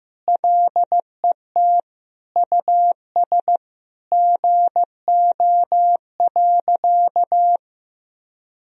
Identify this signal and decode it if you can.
Morse code, heard as a keyed tone, about 15 wpm, 700 Hz. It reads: LET US GO.
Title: Sentences from Top 1000 Words - Review Part 2 - 15wpm